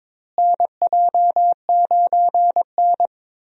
Morse code, heard as a keyed tone, about 22 wpm, 700 Hz. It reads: NJ9N
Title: USA Call Signs - 4 Characters - Repeated 2x - 22wpm